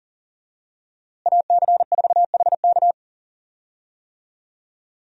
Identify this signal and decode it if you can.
Morse code, heard as a keyed tone, about 40 wpm, 700 Hz. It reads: AC4HK
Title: USA Call Signs - 5 Characters - Repeated 2x - 40wpm